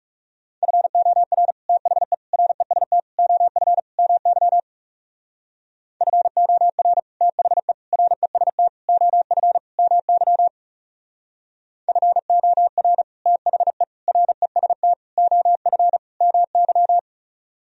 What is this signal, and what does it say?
Morse code, heard as a keyed tone, about 36 wpm, 700 Hz. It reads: FOR THE REST OF MY FOR THE REST OF MY FOR THE REST OF MY
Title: Speed-Racing - Sets of 5 Words 30wpm